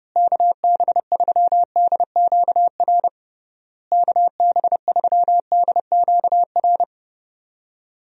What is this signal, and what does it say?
Morse code, heard as a keyed tone, about 30 wpm, 700 Hz. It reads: KB3DQR KB3DQR
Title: USA Call Signs - 6 Characters - Repeated 3x - 30wpm